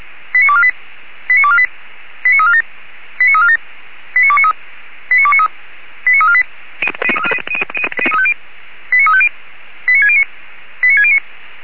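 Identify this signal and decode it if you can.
Five-tone selective call (ZVEI), heard as quick runs of five tones, tones of about 70 ms. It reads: 89248 89248 89347 89347 89293 89293 89348 89350 89609 89609
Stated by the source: ZVEI five-tone selective call received by SDR.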